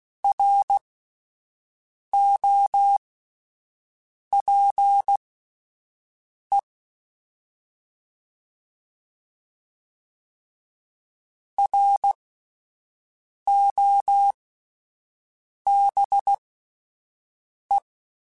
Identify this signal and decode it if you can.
Morse code, heard as a keyed tone, about 16 wpm, spaced as for 5 wpm, 800 Hz. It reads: ROPE ROBE